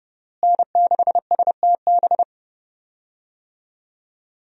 Morse code, heard as a keyed tone, about 30 wpm, 700 Hz. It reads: N6STB
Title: USA Call Signs - 5 Characters - Repeated 2x - 30wpm